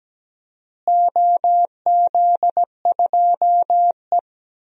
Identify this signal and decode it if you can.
Morse code, heard as a keyed tone, about 17 wpm, 700 Hz. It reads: OZ2E
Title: DXCC-100 Call Signs - 4 Characters - Repeated 3x - 17wpm